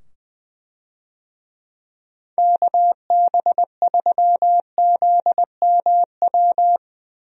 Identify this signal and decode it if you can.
Morse code, heard as a keyed tone, about 21 wpm, 700 Hz. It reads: KB3ZMW
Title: USA Call Signs - 6 Characters - 20wpm